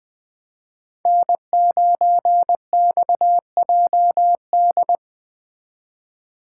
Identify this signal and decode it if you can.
Morse code, heard as a keyed tone, about 19 wpm, 700 Hz. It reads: N9XJD
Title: Speed Racing - USA 1x3 Call Signs 17wpm